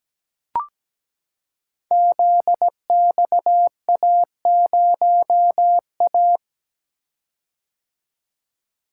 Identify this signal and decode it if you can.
Morse code, heard as a keyed tone, about 17 wpm, 700 Hz. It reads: ZXA0A
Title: Code Groups - 5 Letter:Numbers - 17wpm